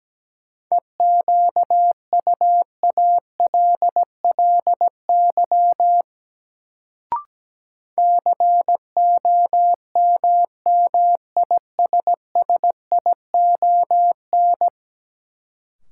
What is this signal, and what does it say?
Morse code, heard as a keyed tone, about 17 wpm, 700 Hz. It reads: EQUALLY COMMISSION
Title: Sentences from Top 1400 Words - New Word Drill - 17wpm